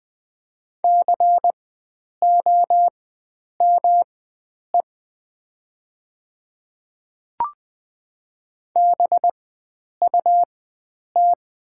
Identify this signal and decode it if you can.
Morse code, heard as a keyed tone, about 20 wpm, spaced as for 9 wpm, 700 Hz. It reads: COME BUT
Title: 4x Character Spacing - Top 100 Words - 20wpm